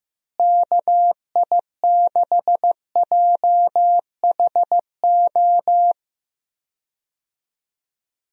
Morse code, heard as a keyed tone, about 15 wpm, 700 Hz. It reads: KI6JHO
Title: USA Call Signs - 6 Characters - Repeated 2x - 15wpm